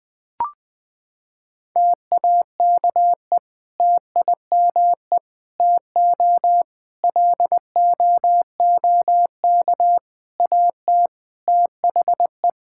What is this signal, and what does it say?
Morse code, heard as a keyed tone, about 20 wpm, 700 Hz. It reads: TAKE TIME TO LOOK AT THE
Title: Sentences with 3-Word Phrases - 20wpm